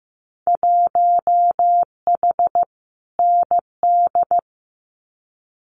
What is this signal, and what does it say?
Morse code, heard as a keyed tone, about 15 wpm, 700 Hz. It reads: 1H ND
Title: Winter Field Day Exchanges 15wpm